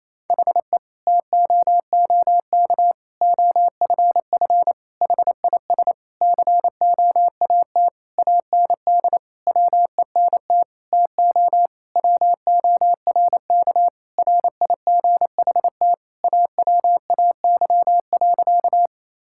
Morse code, heard as a keyed tone, about 28 wpm, 700 Hz. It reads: HE TOOK OFF HIS COAT AND WENT TO WORK RIGHT AWAY.
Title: Sentences from Top 1600 Words 28wpm